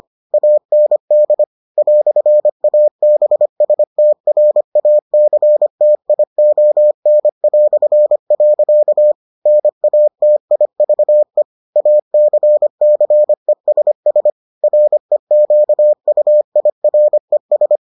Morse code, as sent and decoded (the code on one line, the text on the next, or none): .- -. -.. / .-..-. .- -... ... - .-. .- -.-. - .. --- -. .-..-. .-.-.- / -. .- - .. ...- . / .- -.-. -.-. ... ... / .-. --.- ..- .. .-. ...
AND "ABSTRACTION". NATIVE ACCESS REQUIRES